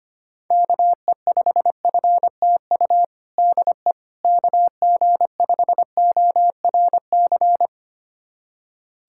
Morse code, heard as a keyed tone, about 25 wpm, 700 Hz. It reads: KE5FTU DE KG5ORC
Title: USA Callsign DE Callsign 25wpm